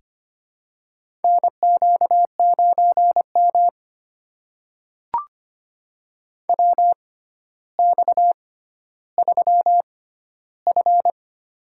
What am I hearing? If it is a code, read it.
Morse code, heard as a keyed tone, about 25 wpm, spaced as for 13 wpm, 700 Hz. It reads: NQ9M WX3F
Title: USA Call Signs - 4 Characters - 6x Character Spacing - 25wpm